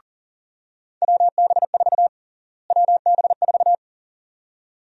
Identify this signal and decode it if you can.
Morse code, heard as a keyed tone, about 37 wpm, 700 Hz. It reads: WB4 WB4
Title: USA Call Sign Prefixes - Repeated 2x - 40wpm